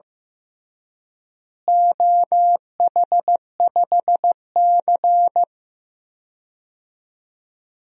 Morse code, heard as a keyed tone, about 15 wpm, 700 Hz. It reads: OH5C
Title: DXCC-100 Call Signs - 4 Characters - Repeated 3x - 15wpm